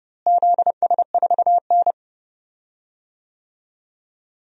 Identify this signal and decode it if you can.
Morse code, heard as a keyed tone, about 30 wpm, 700 Hz. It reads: ZS4N
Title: DXCC-100 Call Signs - 4 Characters - Repeated 3x - 30wpm